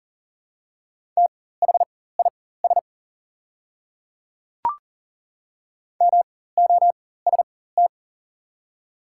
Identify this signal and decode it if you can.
Morse code, heard as a keyed tone, about 40 wpm, spaced as for 19 wpm, 700 Hz. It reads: THIS MOST
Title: 4x Character Spacing - Top 100 Words - 40wpm